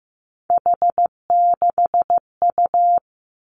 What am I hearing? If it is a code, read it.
Morse code, heard as a keyed tone, about 15 wpm, 700 Hz. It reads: H6U
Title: Three Challenging Letter-Number Combinations 15wpm